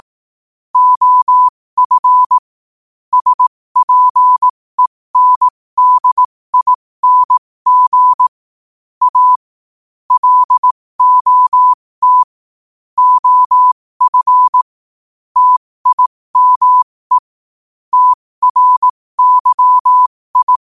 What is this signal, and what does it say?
Morse code, heard as a keyed tone, about 18 wpm, spaced as for 15 wpm, 1000 Hz. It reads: OF SPENDING A LOT OF TIME TRYI
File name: qotdfiles / 15wpm